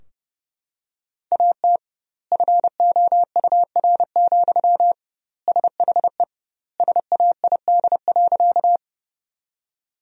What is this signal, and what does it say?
Morse code, heard as a keyed tone, about 30 wpm, 700 Hz. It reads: AT FOUR, SHE SAID.
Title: 2x Word-Spacing - Sentences from Top 300 Words - 30wpm